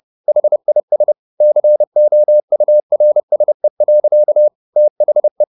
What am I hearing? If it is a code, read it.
Morse code, heard as a keyed tone, about 30 wpm, 600 Hz. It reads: HIS COURSE. THE
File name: the old man and the sea / the old man and the sea 1 30wpm